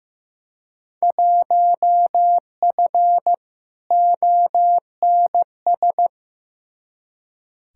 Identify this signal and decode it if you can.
Morse code, heard as a keyed tone, about 15 wpm, 700 Hz. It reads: 1F ONS
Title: Field Day Exchanges Encore - 15wpm